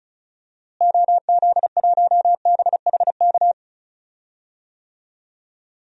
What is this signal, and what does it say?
Morse code, heard as a keyed tone, about 36 wpm, 700 Hz. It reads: OZ1BHK
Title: DXCC-100 Call Signs - 6 Characters - Repeated 3x - 35wpm